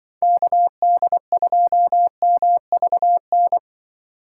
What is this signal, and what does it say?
Morse code, heard as a keyed tone, about 24 wpm, 700 Hz. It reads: KD2MVN
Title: Speed Racing - USA 2x3 Call Signs 20wpm